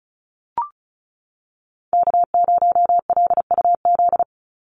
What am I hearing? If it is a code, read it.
Morse code, heard as a keyed tone, about 34 wpm, 700 Hz. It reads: K0LUZ
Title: Call Signs Repeated 2x - 35wpm